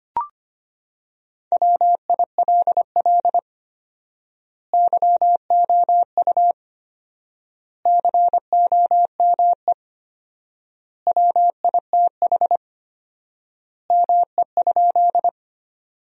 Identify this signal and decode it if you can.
Morse code, heard as a keyed tone, about 25 wpm, 700 Hz. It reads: WILL YOU COME WITH ME?
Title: 4x Word-Spacing - Sentences from Top 200 Words - 25wpm